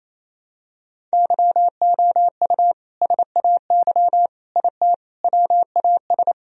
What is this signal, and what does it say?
Morse code, heard as a keyed tone, about 28 wpm, 700 Hz. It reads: YOU SAY IT WAS
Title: Sentences from Top 1000 Words - Review Part 1 - 28wpm